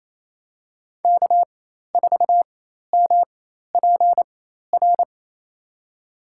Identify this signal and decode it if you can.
Morse code, heard as a keyed tone, about 28 wpm, spaced as for 13 wpm, 700 Hz. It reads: K4MPR
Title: USA Call Signs - 5 Characters - 4x Character Spacing - 28wpm